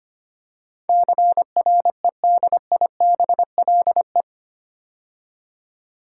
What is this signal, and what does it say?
Morse code, heard as a keyed tone, about 25 wpm, 700 Hz. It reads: CREDIBLE
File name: 8 Letter Words - First 3 Letters Given - 25wpm